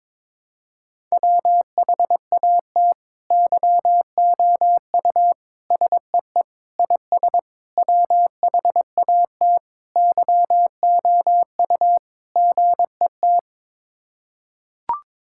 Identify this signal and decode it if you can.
Morse code, heard as a keyed tone, about 22 wpm, 700 Hz. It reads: WHAT YOU SEE IS WHAT YOU GET
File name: practice / 22wpm / Sets of 7 Words - 22wpm